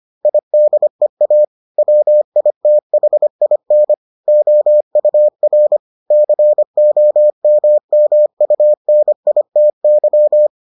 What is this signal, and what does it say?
Morse code, heard as a keyed tone, about 25 wpm, 600 Hz. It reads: IDEA WITHIN OUR COMMUNITY